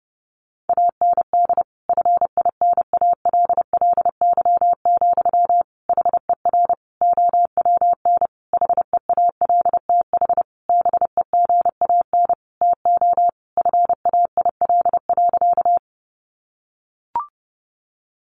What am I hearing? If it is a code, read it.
Morse code, heard as a keyed tone, about 30 wpm, 700 Hz. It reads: AND FINALLY, HER OWN HEALTH BEGAN TO FAIL.
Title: Sentences from Top 1100 Words 30wpm